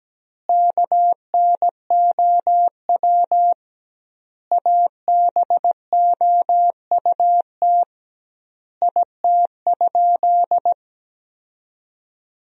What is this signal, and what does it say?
Morse code, heard as a keyed tone, about 17 wpm, 700 Hz. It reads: KNOW ABOUT IT?
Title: 2x Word-Spacing - Sentences from Top 100 Words - 17wpm